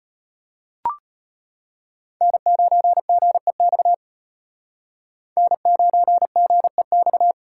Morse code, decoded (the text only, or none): N9GEX N9GEX